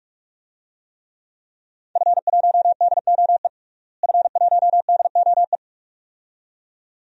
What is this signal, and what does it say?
Morse code, heard as a keyed tone, about 46 wpm, 700 Hz. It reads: F1DOE F1DOE
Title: DXCC-100 Call Signs - 5 Characters - Repeated 2x - 45wpm